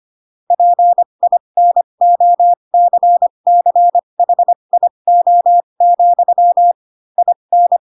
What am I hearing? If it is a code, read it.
Morse code, heard as a keyed tone, about 25 wpm, spaced as for 22 wpm, 700 Hz. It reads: PINOCCHIO, IN
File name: Pinocchio / Pinocchio ch6 18 25 ik1hge